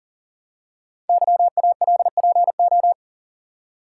Morse code, heard as a keyed tone, about 40 wpm, 700 Hz. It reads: YALPO
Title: Code Groups - 5 Letters - 40wpm